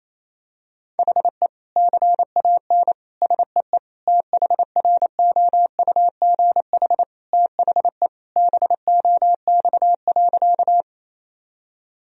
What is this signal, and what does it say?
Morse code, heard as a keyed tone, about 28 wpm, 700 Hz. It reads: HE CAN SEE THROUGH THE BOX.
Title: Sentences from Top 1000 Words - Review Part 2 - 28wpm